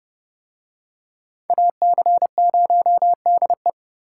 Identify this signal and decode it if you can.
Morse code, heard as a keyed tone, about 28 wpm, 700 Hz. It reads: AC0DE